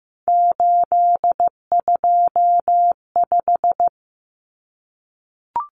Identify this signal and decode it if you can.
Morse code, heard as a keyed tone, about 15 wpm, 700 Hz. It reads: 825
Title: Common Cut Numbers - 3-Digit 15wpm